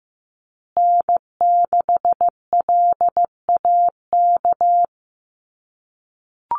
Morse code, heard as a keyed tone, about 15 wpm, 700 Hz. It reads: N6LAK
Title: USA Call Signs Repeated 2x 15wpm